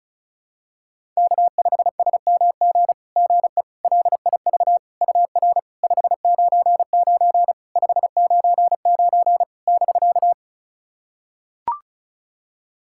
Morse code, heard as a keyed tone, about 36 wpm, 690 Hz. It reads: K5SMG GE LIV UR 599 599 <BK>
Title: SOTA QSO Practice 35wpm